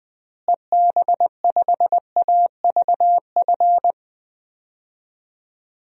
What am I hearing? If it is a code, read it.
Morse code, heard as a keyed tone, about 20 wpm, 700 Hz. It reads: EB5AVF
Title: DXCC-100 Call Signs - 6 Characters - Repeated 2x - 20wpm